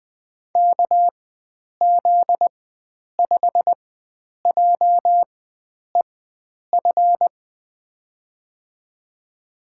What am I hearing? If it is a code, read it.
Morse code, heard as a keyed tone, about 20 wpm, spaced as for 9 wpm, 700 Hz. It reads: KZ5JEF
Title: USA Call Signs - 6 Characters - 4x Character Spacing - 20wpm